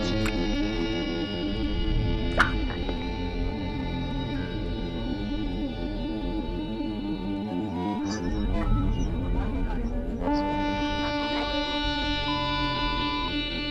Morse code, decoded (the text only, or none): none